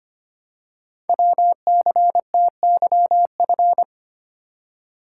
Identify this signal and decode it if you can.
Morse code, heard as a keyed tone, about 25 wpm, 700 Hz. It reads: WCTYF
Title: Code Groups - 5 Letter:Numbers - 25wpm